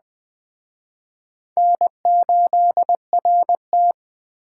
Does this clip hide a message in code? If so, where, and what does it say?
Yes: Morse code from 1.6 to 3.9 s, heard as a keyed tone, about 22 wpm, 700 Hz. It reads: N8RT